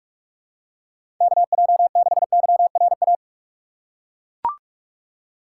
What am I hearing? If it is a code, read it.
Morse code, heard as a keyed tone, about 42 wpm, 700 Hz. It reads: KJ6YRA
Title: USA Call Signs - 6 Characters - 2x Character Spacing - 45wpm